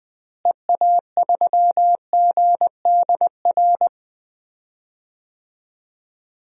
Morse code, heard as a keyed tone, about 20 wpm, 700 Hz. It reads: EA3GDR
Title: DXCC-100 Call Signs - 6 Characters - Repeated 2x - 20wpm